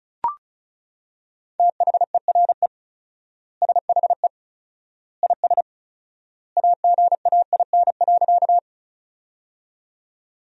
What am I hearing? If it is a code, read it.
Morse code, heard as a keyed tone, about 34 wpm, 700 Hz. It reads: THERE SHE IS AGAIN.
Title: 4x Word-Spacing - Sentences from Top 200 Words - 35wpm